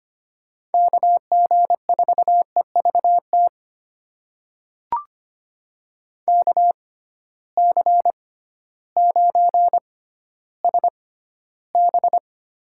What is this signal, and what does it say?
Morse code, heard as a keyed tone, about 25 wpm, spaced as for 13 wpm, 700 Hz. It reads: KG4EVT KC9SB
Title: USA Call Signs - 6 Characters - 6x Character Spacing - 25wpm